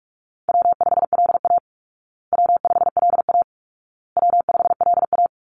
Morse code, heard as a keyed tone, about 44 wpm, 700 Hz. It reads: W5LA W5LA W5LA
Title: Call Signs Repeated 2x - 45wpm